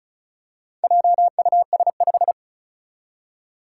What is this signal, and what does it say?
Morse code, heard as a keyed tone, about 36 wpm, 700 Hz. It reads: JUS5
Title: Four Challenging Letter-Number Combinations 35wpm